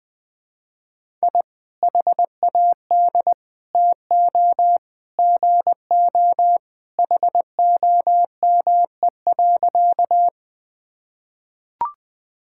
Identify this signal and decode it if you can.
Morse code, heard as a keyed tone, about 20 wpm, 700 Hz. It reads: I HAD TO GO HOME.
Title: Sentences with 3-Word Phrases - 20wpm